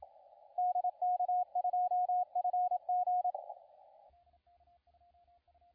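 Morse code, heard as a keyed tone, about 27 wpm, 720 Hz, with radio noise. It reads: DK2FG
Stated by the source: CW contest audio recorded at LA4C during the SAC CW contest.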